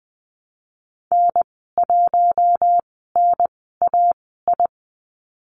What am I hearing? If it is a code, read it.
Morse code, heard as a keyed tone, about 19 wpm, spaced as for 14 wpm, 700 Hz. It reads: N1NAI
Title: 042 Lesson 8g - Number 1 - Call Signs 20wpm